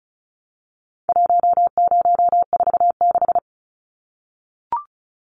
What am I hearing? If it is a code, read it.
Morse code, heard as a keyed tone, about 34 wpm, 700 Hz. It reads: 1046
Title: Numbers - 4-Digit 35wpm